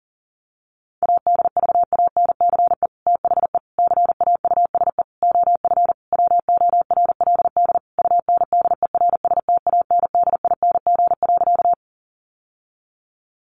Morse code, heard as a keyed tone, about 40 wpm, 700 Hz. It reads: ADVANCE THE CAUSE OF WORLD UNDERSTANDING.
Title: Sentences from Top 1400 Words 40wpm